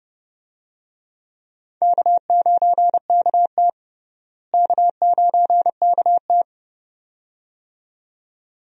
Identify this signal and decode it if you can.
Morse code, heard as a keyed tone, about 32 wpm, 700 Hz. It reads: K9KT K9KT
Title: USA Call Signs - 4 Characters - Repeated 2x - 30wpm